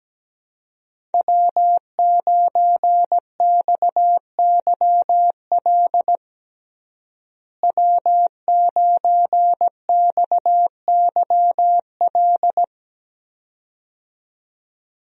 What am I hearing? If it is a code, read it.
Morse code, heard as a keyed tone, about 17 wpm, 700 Hz. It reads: W9XYL W9XYL
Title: USA Call Signs - 5 Characters - Repeated 2x - 17wpm